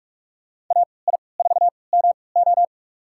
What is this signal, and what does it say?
Morse code, heard as a keyed tone, about 44 wpm, spaced as for 25 wpm, 700 Hz. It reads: AI4MO
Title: USA Call Signs - 5 Characters - 3x Character Spacing - 45wpm